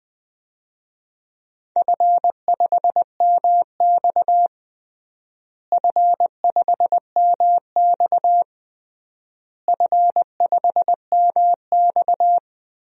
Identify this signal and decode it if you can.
Morse code, heard as a keyed tone, about 21 wpm, 700 Hz. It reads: F5MX F5MX F5MX
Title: DXCC-100 Call Signs - 4 Characters - Repeated 3x - 20wpm